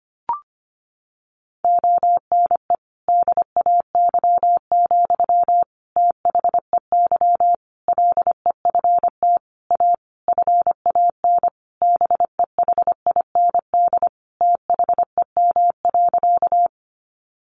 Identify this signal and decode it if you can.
Morse code, heard as a keyed tone, about 25 wpm, 700 Hz. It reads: ONE DAY, THEY LEFT A FAN BEHIND THEM.
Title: Sentences from Top 1300 Words 25wpm